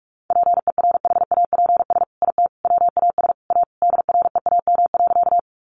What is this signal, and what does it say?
Morse code, heard as a keyed tone, about 44 wpm, 700 Hz. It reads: PERHAPS IT WAS A DREAM.
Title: Sentences from Top 1500 Words 45wpm